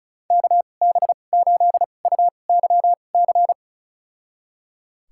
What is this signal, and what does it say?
Morse code, heard as a keyed tone, about 36 wpm, spaced as for 25 wpm, 700 Hz. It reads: KB8UYC
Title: USA Call Signs - 2x Character Spacing - 35wpm